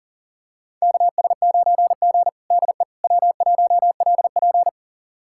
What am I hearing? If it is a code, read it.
Morse code, heard as a keyed tone, about 40 wpm, 700 Hz. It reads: KS9G DE W1LP
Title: USA Callsign DE Callsign - 4 Characters - 40wpm